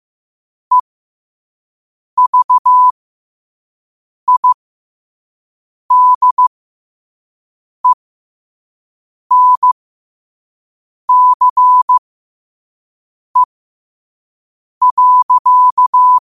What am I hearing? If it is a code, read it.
Morse code, heard as a keyed tone, about 15 wpm, spaced as for 5 wpm, 1000 Hz. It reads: EVIDENCE.
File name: qotdfiles / 5wpm